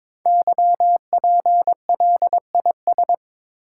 Morse code, heard as a keyed tone, about 22 wpm, 690 Hz. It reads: YPLIS
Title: Code Groups - 5 Letters - 22wpm